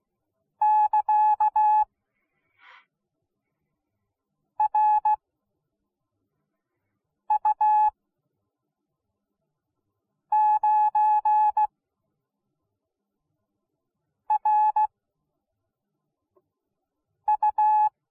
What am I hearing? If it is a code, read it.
Morse code, heard as a keyed tone, about 16 wpm, spaced as for 3 wpm, 850 Hz. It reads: <KA> RU9RU